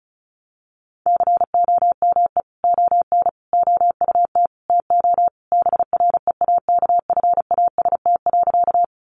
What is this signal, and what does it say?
Morse code, heard as a keyed tone, about 34 wpm, 700 Hz. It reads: COME ON OUT TO BREAKFAST.
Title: Sentences from Top 1500 Words 35wpm